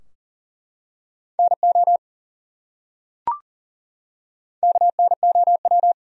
Morse code, heard as a keyed tone, about 40 wpm, 700 Hz. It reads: NO KNOW
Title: Top 100 Words 40wpm